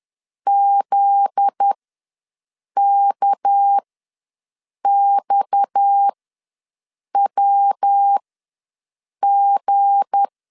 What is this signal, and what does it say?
Morse code, heard as a keyed tone, about 10 wpm, spaced as for 6 wpm, 800 Hz. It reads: ZKXWG
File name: LEZIONI / 7 GF9Z / 7 GF9Z ws 10wpm 3919